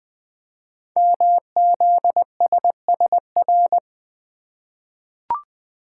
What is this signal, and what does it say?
Morse code, heard as a keyed tone, about 20 wpm, 700 Hz. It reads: MZSSR
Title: Code Groups - 5 Letters - 20wpm